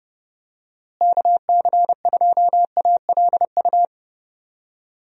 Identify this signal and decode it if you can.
Morse code, heard as a keyed tone, about 32 wpm, 700 Hz. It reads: KC2ALU
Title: USA Call Signs - 6 Characters - Repeated 2x - 30wpm